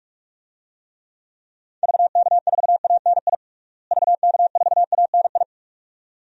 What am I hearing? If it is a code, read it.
Morse code, heard as a keyed tone, about 46 wpm, 700 Hz. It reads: VK4ANI VK4ANI
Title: DXCC-100 Call Signs - 6 Characters - Repeated 2x - 45wpm